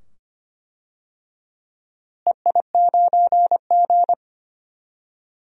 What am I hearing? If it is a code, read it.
Morse code, heard as a keyed tone, about 25 wpm, 700 Hz. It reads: EI9G
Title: DXCC-100 Call Signs - 4 Characters - Repeated 2x - 25wpm